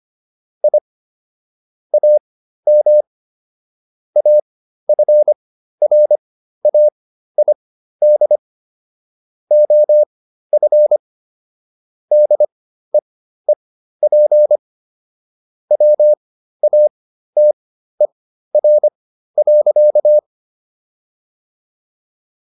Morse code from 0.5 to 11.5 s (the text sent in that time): I AM AFRAID OF